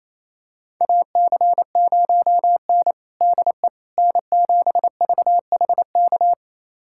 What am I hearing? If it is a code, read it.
Morse code, heard as a keyed tone, about 28 wpm, 700 Hz. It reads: AC0N DE N7VHK
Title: USA Callsign DE Callsign 28wpm